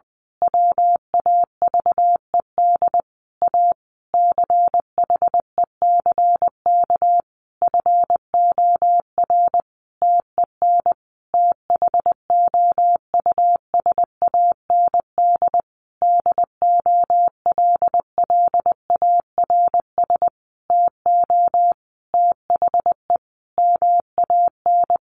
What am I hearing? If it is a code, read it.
Morse code, heard as a keyed tone, about 20 wpm, 700 Hz. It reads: WAVED A CHECK FOR TEN THOUSAND DOLLARS TO THE MAN
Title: A Sound Of Thunder 20wpm